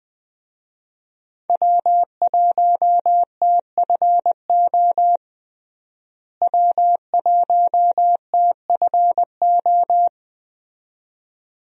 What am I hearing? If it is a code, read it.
Morse code, heard as a keyed tone, about 20 wpm, 700 Hz. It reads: W1TFO W1TFO
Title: USA Call Signs - 5 Characters - Repeated 2x - 20wpm